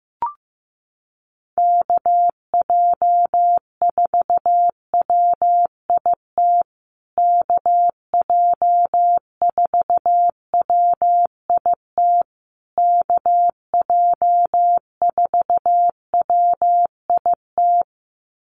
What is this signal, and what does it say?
Morse code, heard as a keyed tone, about 15 wpm, 700 Hz. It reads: KJ4WIT KJ4WIT KJ4WIT
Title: USA Call Signs Repeated 3x 15wpm